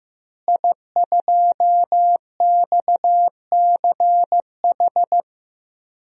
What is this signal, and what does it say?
Morse code, heard as a keyed tone, about 15 wpm, 700 Hz. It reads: I2XCH
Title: DXCC-100 Call Signs - 5 Characters - Repeated 3x - 15wpm